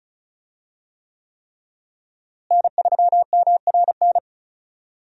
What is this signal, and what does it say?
Morse code, heard as a keyed tone, about 34 wpm, 700 Hz. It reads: N3MRN
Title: USA Call Signs - 5 Characters - Repeated 3x - 35wpm